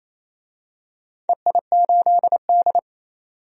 Morse code, heard as a keyed tone, about 28 wpm, 700 Hz. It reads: EI8D
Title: DXCC-100 Call Signs - 4 Characters - 28wpm